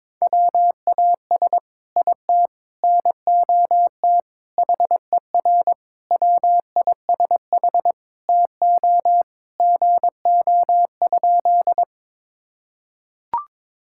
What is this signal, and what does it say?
Morse code, heard as a keyed tone, about 22 wpm, 700 Hz. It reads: WAS IT NOT HER WISH TO GO?
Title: Sentences from Top 1600 Words 22wpm